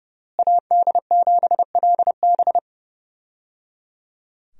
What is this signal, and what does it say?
Morse code, heard as a keyed tone, about 30 wpm, 700 Hz. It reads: AD7LB
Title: USA Call Signs - 5 Characters - Repeated 3x - 30wpm